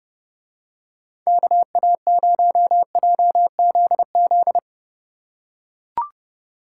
Morse code, heard as a keyed tone, about 27 wpm, 700 Hz. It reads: KA0JZZ